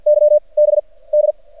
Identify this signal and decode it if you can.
Morse code, heard as a keyed tone, about 53 wpm, 600 Hz. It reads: Y B D
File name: archived / CRSnet